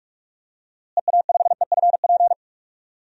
Morse code, heard as a keyed tone, about 46 wpm, 700 Hz. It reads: EA5EFP
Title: DXCC-100 Call Signs - 6 Characters - Repeated 2x - 45wpm